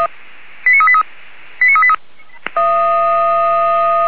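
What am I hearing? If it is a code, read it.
Five-tone selective call (ZVEI), heard as quick runs of five tones, tones of about 70 ms. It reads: 89383 89383
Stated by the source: ZVEI five-tone selective call received by SDR.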